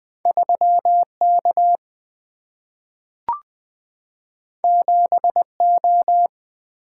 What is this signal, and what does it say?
Morse code, heard as a keyed tone, about 20 wpm, 700 Hz. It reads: 3K 7O
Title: Single-Letter Single-Number 20wpm